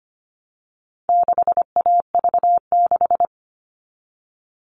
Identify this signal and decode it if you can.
Morse code, heard as a keyed tone, about 25 wpm, 700 Hz. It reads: 6AV6
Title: All Cut Numbers - 4-Digit 25wpm